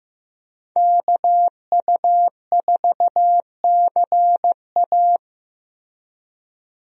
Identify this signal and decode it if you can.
Morse code, heard as a keyed tone, about 15 wpm, 700 Hz. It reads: KU4CA
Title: USA Call Signs - 5 Characters - Repeated 3x - 15wpm